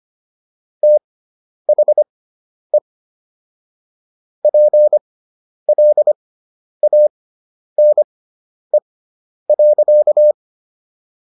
Morse code, heard as a keyed tone, about 25 wpm, spaced as for 10 wpm, 600 Hz. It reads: THE PLANE.